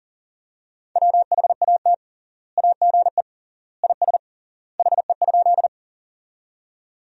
Morse code, heard as a keyed tone, about 40 wpm, 700 Hz. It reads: WHAT AGE IS HE?
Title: 3x Word-Spacing - Sentences from Top 500 Words - 40wpm